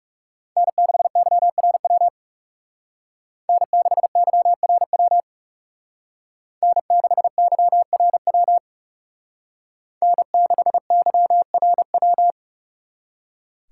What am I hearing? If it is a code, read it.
Morse code, heard as a keyed tone, about 34 wpm, 700 Hz. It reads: N6YRW N6YRW N6YRW N6YRW
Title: Speed Racing - USA 1x3 Call Signs 30wpm